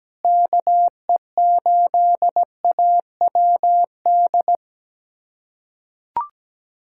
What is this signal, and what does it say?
Morse code, heard as a keyed tone, about 17 wpm, 700 Hz. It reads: KE8AWD
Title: USA Callsign DE Callsign - 6 Characters - 17wpm